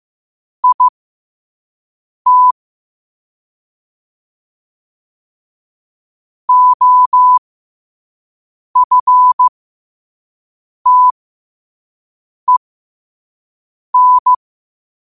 Morse code, heard as a keyed tone, about 15 wpm, spaced as for 5 wpm, 1000 Hz. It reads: IT OFTEN
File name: qotdfiles / 5wpm